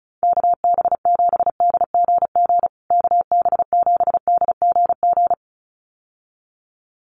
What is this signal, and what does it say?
Morse code, heard as a keyed tone, about 34 wpm, 700 Hz. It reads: KB7DGG KB7DGG
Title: USA Call Signs Repeated 2x 35wpm